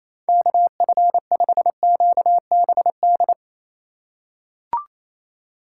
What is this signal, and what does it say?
Morse code, heard as a keyed tone, about 28 wpm, 700 Hz. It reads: KF5QBD
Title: USA Call Signs - 8x Character Spacing - 28wpm